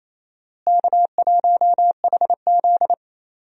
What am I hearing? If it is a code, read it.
Morse code, heard as a keyed tone, about 28 wpm, 700 Hz. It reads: K1HZ
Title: USA Call Signs - 4 Characters - Repeated 2x - 28wpm